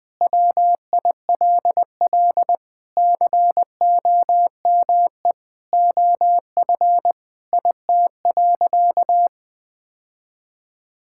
Morse code, heard as a keyed tone, about 20 wpm, 700 Hz. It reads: WILL COME OF IT.
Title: Sentences with 2-Word Phrases - 20wpm